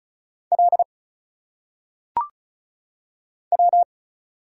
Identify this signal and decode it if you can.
Morse code, heard as a keyed tone, about 36 wpm, 700 Hz. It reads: L W
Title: Single Letters - Phonetic Pronunciation 35wpm